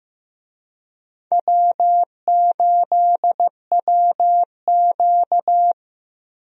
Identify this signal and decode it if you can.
Morse code, heard as a keyed tone, about 15 wpm, 700 Hz. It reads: W8WQ
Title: USA Call Signs - 4 Characters - Repeated 2x - 15wpm